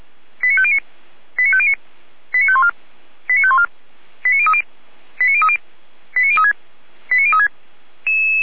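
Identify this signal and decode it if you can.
Five-tone selective call (ZVEI), heard as quick runs of five tones, tones of about 70 ms. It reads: 89509 89509 89514 89514 89030 89030 89036 89036
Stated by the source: ZVEI five-tone selective call 89010 received by SDR at feuerwehr-orlamuende.org, fire brigade.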